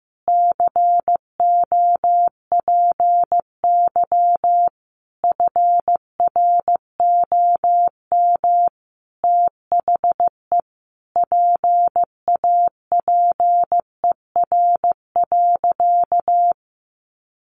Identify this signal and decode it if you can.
Morse code, heard as a keyed tone, about 15 wpm, 700 Hz. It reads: COPY FROM THE PAPER.
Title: Sentences from Top 1400 Words 15wpm